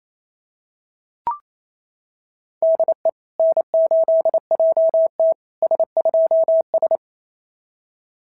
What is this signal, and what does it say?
Morse code, heard as a keyed tone, about 27 wpm, 650 Hz. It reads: DE N8JT S2S
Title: SOTA QSO Practice 28wpm